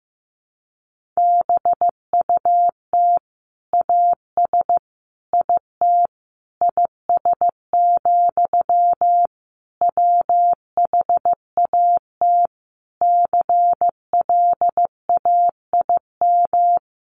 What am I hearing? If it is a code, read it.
Morse code, heard as a keyed tone, about 15 wpm, 700 Hz. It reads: BUT AS IT IS, WHAT CLAIM
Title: Sentences from Top 1500 Words 15wpm